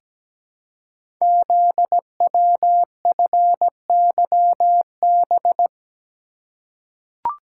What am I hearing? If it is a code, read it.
Morse code, heard as a keyed tone, about 17 wpm, 700 Hz. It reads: ZWFYB
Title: Code Groups - 5 Letters - 17wpm